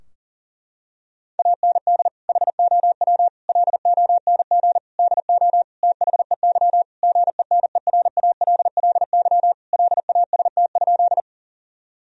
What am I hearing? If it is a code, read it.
Morse code, heard as a keyed tone, about 40 wpm, 700 Hz. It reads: AND HOW LONG DO THEY GENERALLY LAST?
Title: Sentences from Top 1000 Words - Review Part 4 - 40wpm